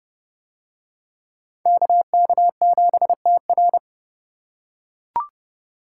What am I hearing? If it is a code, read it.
Morse code, heard as a keyed tone, about 28 wpm, 700 Hz. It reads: KK7TR